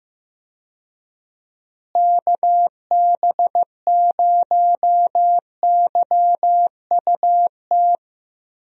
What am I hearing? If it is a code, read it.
Morse code, heard as a keyed tone, about 15 wpm, 700 Hz. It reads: KB0YUT
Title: USA Call Signs - 6 Characters - Repeated 2x - 15wpm